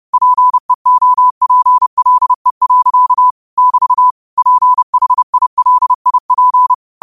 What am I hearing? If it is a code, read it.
Morse code, heard as a keyed tone, about 30 wpm, 1000 Hz. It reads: PEOPLE. <BT> PHILIP